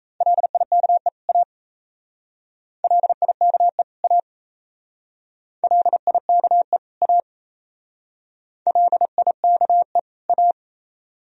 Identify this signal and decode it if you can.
Morse code, heard as a keyed tone, about 34 wpm, 700 Hz. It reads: LIKE A LIKE A LIKE A LIKE A
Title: Speed-Racing - Sets of 2 Words 28wpm